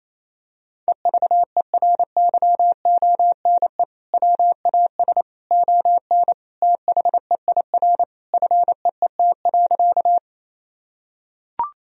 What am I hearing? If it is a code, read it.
Morse code, heard as a keyed tone, about 28 wpm, 700 Hz. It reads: EVERYONE WAS ON THEIR FEET.
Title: Sentences from Top 1000 Words - Review Part 1 - 28wpm